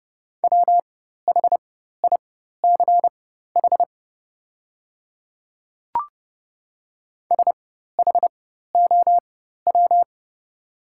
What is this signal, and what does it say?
Morse code, heard as a keyed tone, about 30 wpm, spaced as for 14 wpm, 700 Hz. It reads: WHICH SHOW
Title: 4x Character Spacing - Top 200 Words - 30wpm